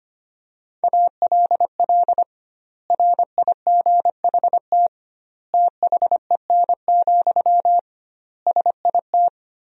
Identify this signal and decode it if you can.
Morse code, heard as a keyed tone, about 25 wpm, 700 Hz. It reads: ALL RIGHT THEN, SIT
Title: 2x Word-Spacing - Sentences from Top 500 Words - 25wpm